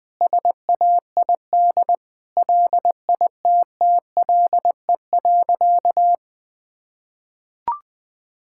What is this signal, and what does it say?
Morse code, heard as a keyed tone, about 19 wpm, 700 Hz. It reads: SAID LITTLE.
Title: Sentences from Top 1600 Words 20wpm